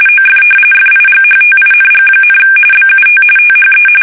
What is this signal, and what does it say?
RTTY, about 45.45 baud. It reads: OF FIGHTING, BUT THE
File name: fsk850 / usnavy1